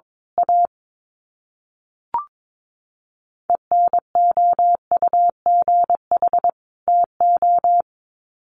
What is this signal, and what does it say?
Morse code, heard as a keyed tone, about 22 wpm, 700 Hz. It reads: A ENOUGH TO
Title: Sets of 2 Words 22wpm